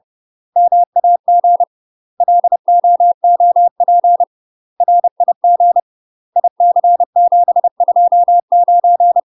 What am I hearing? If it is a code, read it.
Morse code, heard as a keyed tone, about 30 wpm, 700 Hz. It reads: MAG LOOP RIG IC729
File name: UK MASTER QSOs TRAINING / 30WPM / UK MASTER QSOs 0031